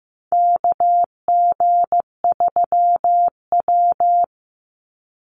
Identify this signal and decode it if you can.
Morse code, heard as a keyed tone, about 15 wpm, 700 Hz. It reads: KG3W
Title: NA SOTA Chaser Callsigns 15wpm